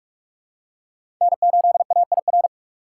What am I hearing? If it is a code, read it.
Morse code, heard as a keyed tone, about 46 wpm, 700 Hz. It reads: N8AIR
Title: USA Call Signs - 5 Characters - 2x Character Spacing - 45wpm